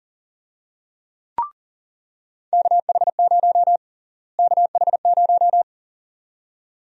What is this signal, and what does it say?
Morse code, heard as a keyed tone, about 40 wpm, 700 Hz. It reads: KH0 KH0
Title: USA Call Sign Prefixes - Repeated 2x - 40wpm